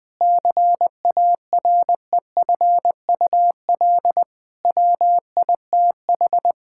Morse code, heard as a keyed tone, about 21 wpm, 700 Hz. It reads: CAREFUL WITH
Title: Sentences from Top 1600 Words 20wpm